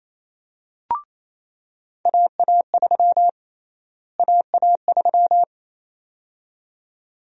Morse code, heard as a keyed tone, about 28 wpm, 700 Hz. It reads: AA3 AA3
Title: USA Call Sign Prefixes - Repeated 2x - 28wpm